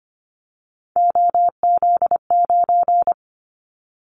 Morse code, heard as a keyed tone, about 25 wpm, 700 Hz. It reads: OZ9
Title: Callsign Prefixes for Top 50 DXCC - 25wpm